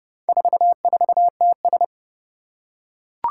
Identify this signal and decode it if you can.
Morse code, heard as a keyed tone, about 30 wpm, 700 Hz. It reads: 44TS
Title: Four Challenging Letter-Number Combinations 30wpm